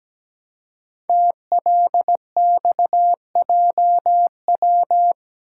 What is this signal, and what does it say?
Morse code, heard as a keyed tone, about 17 wpm, 700 Hz. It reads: TLXJW
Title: Code Groups - 5 Letters - 17wpm